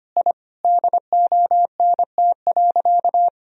Morse code, heard as a keyed tone, about 25 wpm, 700 Hz. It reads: I DONT.
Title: Sentences from Top 1000 Words - Review Part 3 - 25wpm